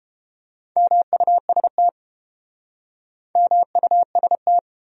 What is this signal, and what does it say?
Morse code, heard as a keyed tone, about 31 wpm, 700 Hz. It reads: MUST MUST